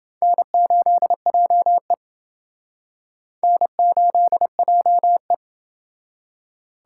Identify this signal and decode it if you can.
Morse code, heard as a keyed tone, about 28 wpm, 700 Hz. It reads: N8JE N8JE